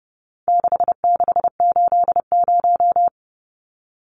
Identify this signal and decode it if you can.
Morse code, heard as a keyed tone, about 30 wpm, 700 Hz. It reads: 6680
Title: Numbers - 4-Digit 30wpm